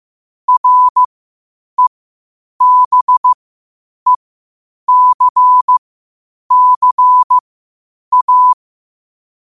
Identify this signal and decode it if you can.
Morse code, heard as a keyed tone, about 15 wpm, spaced as for 8 wpm, 1000 Hz. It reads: REBECCA